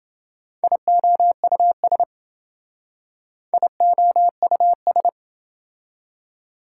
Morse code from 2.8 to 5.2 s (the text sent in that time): IOUS